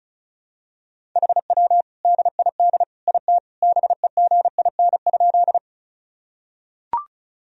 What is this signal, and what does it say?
Morse code, heard as a keyed tone, about 34 wpm, 700 Hz. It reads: HW DID IT BEGIN?
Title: Sentences from Top 300 Words and Abbreviations 35wpm